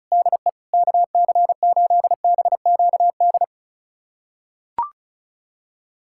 Morse code, heard as a keyed tone, about 36 wpm, 700 Hz. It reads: DE KC8BQD
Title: USA Callsign DE Callsign - 6 Characters - 35wpm